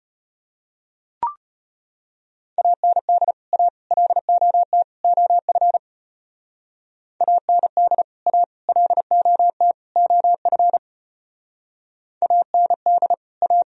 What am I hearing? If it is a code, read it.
Morse code, heard as a keyed tone, about 36 wpm, 700 Hz. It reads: AND A LOT OF AND A LOT OF AND A